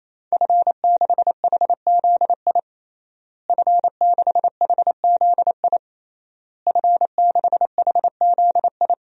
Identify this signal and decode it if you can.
Morse code, heard as a keyed tone, about 28 wpm, 700 Hz. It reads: F6HZI F6HZI F6HZI
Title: DXCC-100 Call Signs - 5 Characters - Repeated 3x - 28wpm